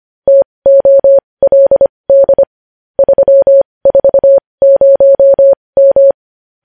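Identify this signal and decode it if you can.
Morse code, heard as a keyed tone, about 25 wpm, spaced as for 20 wpm, 550 Hz. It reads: TOLD 340M
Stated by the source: Morse code news headlines sent at 20.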